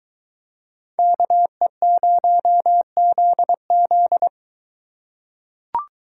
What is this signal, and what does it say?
Morse code, heard as a keyed tone, about 21 wpm, 700 Hz. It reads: KE0ZZ